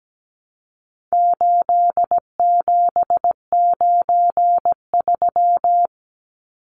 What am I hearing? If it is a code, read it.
Morse code, heard as a keyed tone, about 17 wpm, 700 Hz. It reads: 8793
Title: Numbers - 4-Digit 17wpm